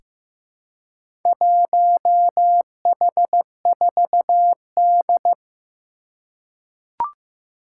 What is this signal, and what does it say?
Morse code, heard as a keyed tone, about 15 wpm, 700 Hz. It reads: 1H4D
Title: Four Challenging Letter-Number Combinations 15wpm